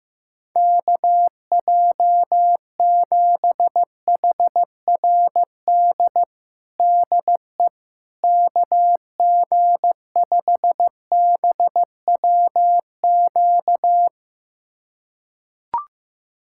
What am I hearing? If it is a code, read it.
Morse code, heard as a keyed tone, about 15 wpm, 700 Hz. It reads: KJ7HRD DE KG5BWQ
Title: USA Callsign DE Callsign 15wpm